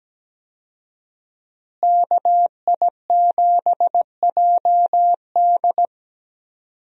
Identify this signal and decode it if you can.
Morse code, heard as a keyed tone, about 17 wpm, 700 Hz. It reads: KI7JD
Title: USA Call Signs - 5 Characters - Repeated 2x - 17wpm